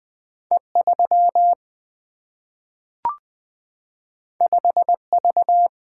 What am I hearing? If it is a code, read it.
Morse code, heard as a keyed tone, about 20 wpm, 700 Hz. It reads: E3 5V
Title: Single-Letter Single-Number 20wpm